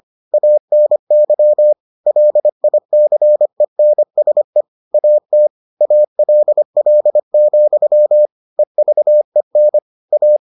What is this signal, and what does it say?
Morse code, heard as a keyed tone, about 25 wpm, 600 Hz. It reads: ANY LICENSE AT ALL, EVEN A